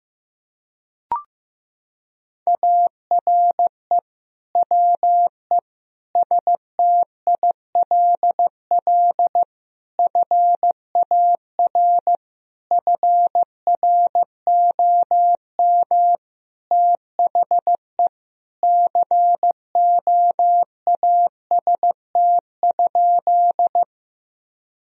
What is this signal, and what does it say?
Morse code, heard as a keyed tone, about 15 wpm, 700 Hz. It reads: ARE WE STILL FAR FROM THE COAST?
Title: Sentences from Top 1000 Words - Review Part 4 - 15wpm